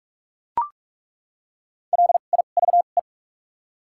Morse code, heard as a keyed tone, about 42 wpm, spaced as for 31 wpm, 700 Hz. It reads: LIVE
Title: 2x Character Spacing - Top 500 Words - 45wpm